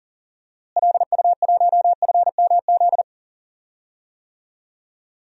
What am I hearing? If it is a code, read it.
Morse code, heard as a keyed tone, about 40 wpm, 700 Hz. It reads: LU1FMZ
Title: DXCC-100 Call Signs - 6 Characters - Repeated 3x - 40wpm